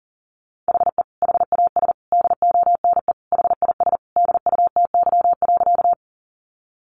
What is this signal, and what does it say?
Morse code, heard as a keyed tone, about 40 wpm, 700 Hz. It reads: HE HAS DONE HIS DUTY.
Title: Sentences from Top 1500 Words 40wpm